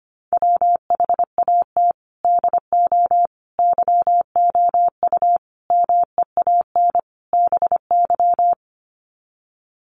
Morse code, heard as a keyed tone, about 25 wpm, 700 Hz. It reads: WHAT DO YOU MEAN BY
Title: Sets of 5 Words 25wpm